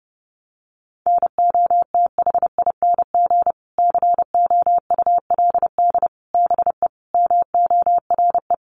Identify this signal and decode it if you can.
Morse code, heard as a keyed tone, about 30 wpm, 700 Hz. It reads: NOTHING COULD BE MORE
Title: Sentences from Top 500 Words 30wpm